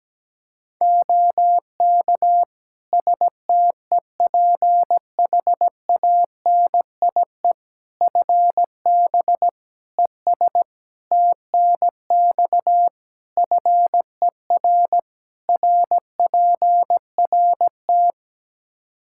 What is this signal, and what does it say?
Morse code, heard as a keyed tone, about 17 wpm, 700 Hz. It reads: OK STEPHANIE FB ES TNX FER RPRT
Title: Practice QSO - Exchange 2 - Info - 17wpm